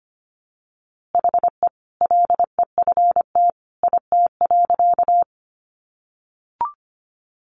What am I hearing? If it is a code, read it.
Morse code, heard as a keyed tone, about 25 wpm, 700 Hz. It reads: HE LEFT IT.
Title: A Sound Of Thunder 25wpm